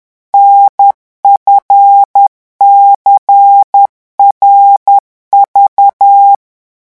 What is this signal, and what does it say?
Morse code, heard as a keyed tone, about 10 wpm, 800 Hz. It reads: NFCRV